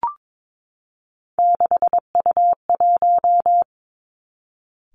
Morse code, heard as a keyed tone, about 22 wpm, 700 Hz. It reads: 6U1
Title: Three Challenging Letter-Number Combinations 22wpm